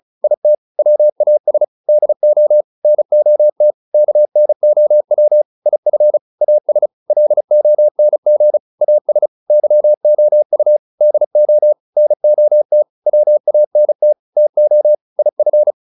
Morse code, heard as a keyed tone, about 34 wpm, 600 Hz. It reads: IT WAS DO NOT KNOW IF AS LONG AS YOU DO NOT WANT TO IF